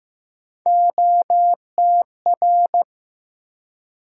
Morse code, heard as a keyed tone, about 16 wpm, 700 Hz. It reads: OTR
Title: USA Call Sign Postfixes - 3 Letters 15wpm